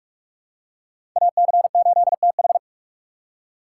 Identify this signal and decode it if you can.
Morse code, heard as a keyed tone, about 44 wpm, 690 Hz. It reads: AC8TH
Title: USA Call Signs - 5 Characters - 6x Character Spacing - 45wpm